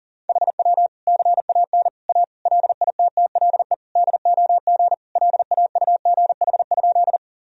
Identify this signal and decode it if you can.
Morse code, heard as a keyed tone, about 40 wpm, 700 Hz. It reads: HW CAN A LITTLE DOG LAUGH?
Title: Sentences from Top 500 Words and Abbreviations 40wpm